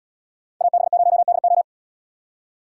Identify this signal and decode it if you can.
Morse code, heard as a keyed tone, about 77 wpm, 700 Hz. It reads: AB0DO